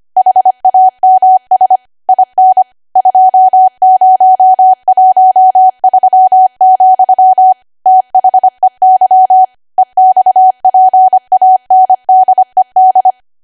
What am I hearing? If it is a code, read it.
Morse code, heard as a keyed tone, about 25 wpm, 750 Hz. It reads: HAMS IN 2013, THEY EXPANDED
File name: Archive / 25 WPM / 25WPM